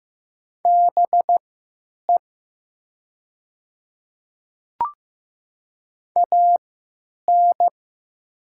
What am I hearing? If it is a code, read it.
Morse code, heard as a keyed tone, about 15 wpm, spaced as for 9 wpm, 700 Hz. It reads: BE AN
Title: 3x Character Spacing - Top 100 Words - 15wpm